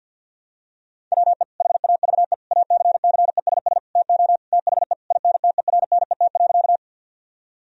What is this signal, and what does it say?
Morse code, heard as a keyed tone, about 50 wpm, 700 Hz. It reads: WE HAVE ACCESS TO THE INTERNET.
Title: Sentences from Top 1000 Words - Review Part 4 - 50wpm